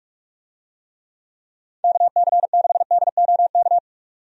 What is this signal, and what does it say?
Morse code, heard as a keyed tone, about 47 wpm, 700 Hz. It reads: KC6DOK
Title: USA Call Signs - 6 Characters - 45wpm